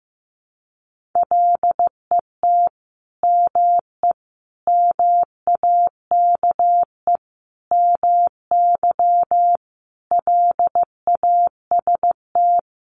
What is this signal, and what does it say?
Morse code, heard as a keyed tone, about 15 wpm, 700 Hz. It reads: LET ME MAKE MY LAST
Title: Sentences from Top 1300 Words 15wpm